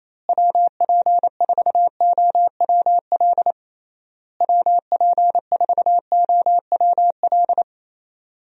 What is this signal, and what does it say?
Morse code, heard as a keyed tone, about 28 wpm, 700 Hz. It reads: WP4OWL WP4OWL
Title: USA Call Signs - 6 Characters - Repeated 3x - 28wpm